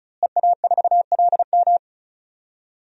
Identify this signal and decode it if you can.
Morse code, heard as a keyed tone, about 34 wpm, 700 Hz. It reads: EA4LM
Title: DXCC-100 Call Signs - 5 Characters - Repeated 3x - 35wpm